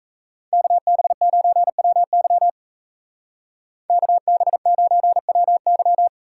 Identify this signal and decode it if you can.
Morse code, heard as a keyed tone, about 37 wpm, 700 Hz. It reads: KB9WY KB9WY